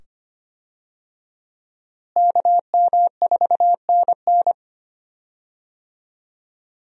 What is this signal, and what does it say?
Morse code, heard as a keyed tone, about 25 wpm, 700 Hz. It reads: KM4NN
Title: USA Call Signs - 5 Characters - Repeated 2x - 25wpm